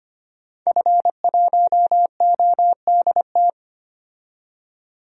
Morse code, heard as a keyed tone, about 25 wpm, 700 Hz. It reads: F1ODT
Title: DXCC-100 Call Signs - 5 Characters - 25wpm